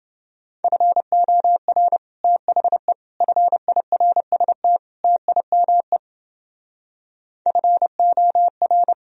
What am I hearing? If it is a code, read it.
Morse code, heard as a keyed tone, about 29 wpm, 700 Hz. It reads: FOR THE FIRST TIME FOR